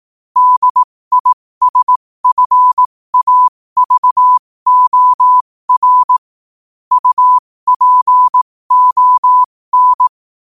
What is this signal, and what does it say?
Morse code, heard as a keyed tone, about 18 wpm, spaced as for 15 wpm, 1000 Hz. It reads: DISFAVOR UPON
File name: qotdfiles / 15wpm